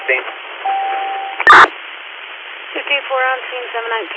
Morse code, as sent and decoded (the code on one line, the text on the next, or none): none